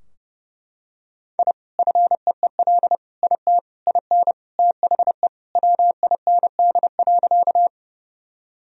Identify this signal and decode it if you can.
Morse code, heard as a keyed tone, about 32 wpm, 700 Hz. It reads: I FEEL IT IN THE WIND.
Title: Sentences from Top 1000 Words - Review Part 2 - 30wpm